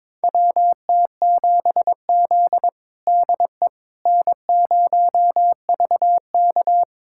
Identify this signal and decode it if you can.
Morse code, heard as a keyed tone, about 22 wpm, 700 Hz. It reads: WT7Z DE N0VK
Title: USA Callsign DE Callsign - 4 Characters - 22wpm